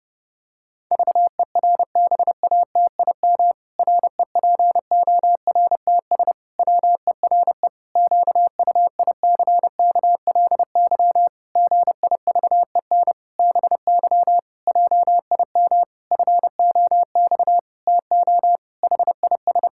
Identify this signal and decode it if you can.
Morse code, heard as a keyed tone, about 28 wpm, 700 Hz. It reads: VERBATIM REPORTS WERE QUICKLY GIVEN BY JIM FOX TO HIS
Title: Pangram Challenge v2 - 30wpm